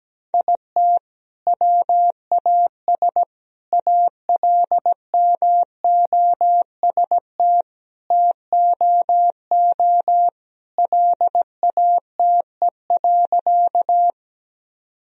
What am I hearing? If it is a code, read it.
Morse code, heard as a keyed tone, about 17 wpm, 700 Hz. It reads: IT WAS ALMOST TOO LATE.
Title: Sentences from Top 1000 Words - Review Part 4 - 17wpm